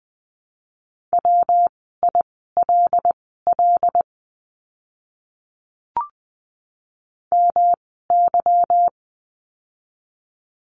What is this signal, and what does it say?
Morse code, heard as a keyed tone, about 20 wpm, spaced as for 14 wpm, 700 Hz. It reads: WILL MY
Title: 192 Lesson 31f - Letter v - Words 20wpm